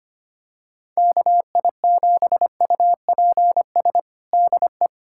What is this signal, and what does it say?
Morse code, heard as a keyed tone, about 25 wpm, 700 Hz. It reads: KI7UPS DE
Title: USA Callsign DE Callsign - 6 Characters - 25wpm